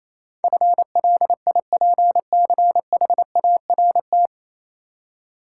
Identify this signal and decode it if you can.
Morse code, heard as a keyed tone, about 27 wpm, 700 Hz. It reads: FLIPCHART
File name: Easy Words - Speed-Racing with Hints - 22wpm